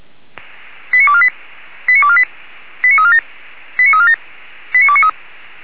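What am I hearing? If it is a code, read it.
Five-tone selective call (ZVEI), heard as quick runs of five tones, tones of about 70 ms. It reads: 89248 89248 89347 89347 89293
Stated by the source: ZVEI five-tone selective call 89406 received on SDR.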